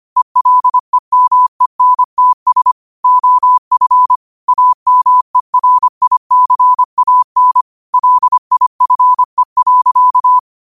Morse code, heard as a keyed tone, about 25 wpm, 1000 Hz. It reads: ELEMENTS OF AMERICAN LIFE.
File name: qotdfiles / 25wpm